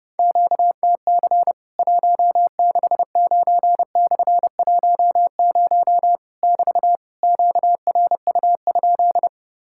Morse code, heard as a keyed tone, about 30 wpm, 700 Hz. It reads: QTC 169/10 <BT> QRU?
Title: Worked All Europe DX Contest - QTC Exchange - 30wpm